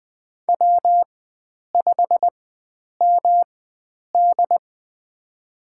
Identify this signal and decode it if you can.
Morse code, heard as a keyed tone, about 20 wpm, spaced as for 9 wpm, 700 Hz. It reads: W5MD
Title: USA Call Signs - 4 Characters - 4x Character Spacing - 20wpm